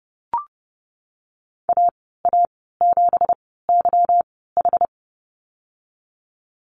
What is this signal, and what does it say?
Morse code, heard as a keyed tone, about 30 wpm, spaced as for 17 wpm, 700 Hz. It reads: AA7YH
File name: Lesson 37g - Letter x - Call Signs 30wpm